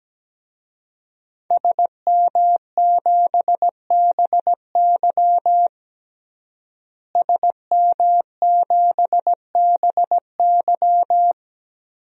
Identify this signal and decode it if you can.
Morse code, heard as a keyed tone, about 17 wpm, 700 Hz. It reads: SM7BY SM7BY
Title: DXCC-100 Call Signs - 5 Characters - Repeated 2x - 17wpm